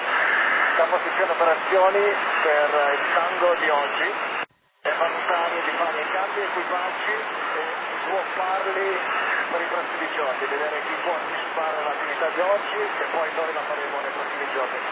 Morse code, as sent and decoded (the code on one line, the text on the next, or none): none